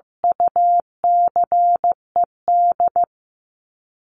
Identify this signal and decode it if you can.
Morse code, heard as a keyed tone, about 15 wpm, 700 Hz. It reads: UCED
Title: ED Suffix - Top 100 Words with Base Words 15wpm